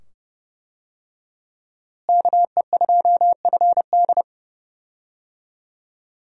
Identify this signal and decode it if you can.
Morse code, heard as a keyed tone, about 31 wpm, 700 Hz. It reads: KE2FD
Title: USA Call Signs - 5 Characters - Repeated 2x - 30wpm